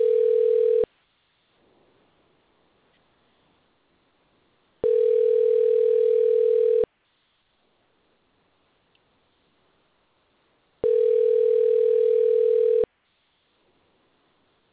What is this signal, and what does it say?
Ringback tone (North America), 440 + 480 Hz.